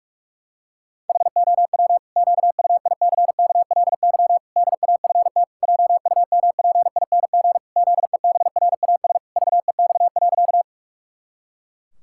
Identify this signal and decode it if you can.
Morse code, heard as a keyed tone, about 46 wpm, 700 Hz. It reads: HOW QUICKLY DAFT JUMPING ZEBRAS VEX.
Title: Pangram Challenge v2 - 45wpm